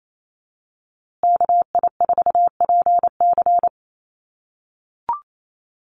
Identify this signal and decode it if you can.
Morse code, heard as a keyed tone, about 28 wpm, 700 Hz. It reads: KI4PC
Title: USA Call Signs Repeated 3x 28wpm